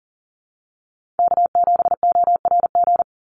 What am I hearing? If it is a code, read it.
Morse code, heard as a keyed tone, about 40 wpm, 700 Hz. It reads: K7ORG
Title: NA SOTA Chaser Callsigns 40wpm